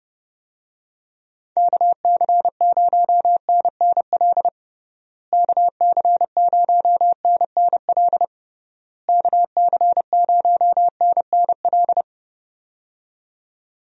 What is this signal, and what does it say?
Morse code, heard as a keyed tone, about 30 wpm, 700 Hz. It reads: KC0NNL KC0NNL KC0NNL
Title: USA Call Signs - 6 Characters - Repeated 3x - 30wpm